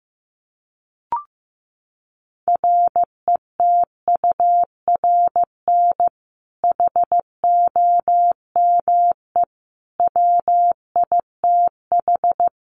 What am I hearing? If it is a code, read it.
Morse code, heard as a keyed tone, about 15 wpm, 700 Hz. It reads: RETURN HOME WITH
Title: Sentences from Top 800 Words 15wpm